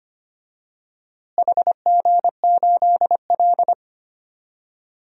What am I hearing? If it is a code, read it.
Morse code, heard as a keyed tone, about 25 wpm, 700 Hz. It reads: HG8L
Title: DXCC-100 Call Signs - 4 Characters - Repeated 3x - 25wpm